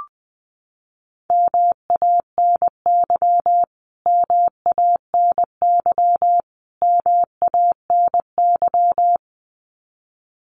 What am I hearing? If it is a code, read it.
Morse code, heard as a keyed tone, about 20 wpm, 700 Hz. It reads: MANY MANY MANY
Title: Top 200 Words Repeated 20wpm